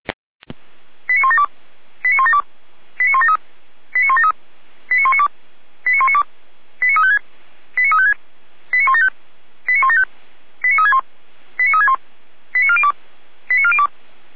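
Five-tone selective call (ZVEI), heard as quick runs of five tones, tones of about 70 ms. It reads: 89172 89172 89173 89173 89192 89192 89356 89356 89175 89175 89361 89361 89402 89402
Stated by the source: ZVEI five-tone call received by SDR.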